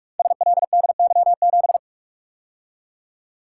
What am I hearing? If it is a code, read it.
Morse code, heard as a keyed tone, about 44 wpm, 700 Hz. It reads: SLDY7
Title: Code Groups - 5 Letter:Numbers - 45wpm